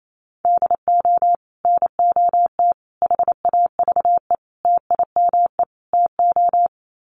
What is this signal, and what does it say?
Morse code, heard as a keyed tone, about 28 wpm, 700 Hz. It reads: DO NOT HAVE TIME TO
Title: Sets of 5 Words 28wpm